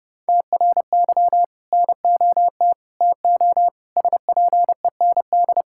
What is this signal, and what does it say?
Morse code, heard as a keyed tone, about 30 wpm, 700 Hz. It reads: TRY NOT TO SPEND
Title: Sentences from Top 1000 Words - Review Part 5 - 30wpm